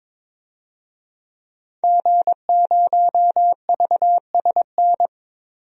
Morse code, heard as a keyed tone, about 22 wpm, 700 Hz. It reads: G0VSN
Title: DXCC-100 Call Signs - 5 Characters - 22wpm